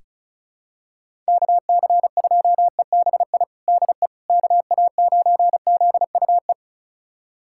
Morse code, heard as a keyed tone, about 36 wpm, 700 Hz. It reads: KC2EBI DE KA9ZUE
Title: USA Callsign DE Callsign 35wpm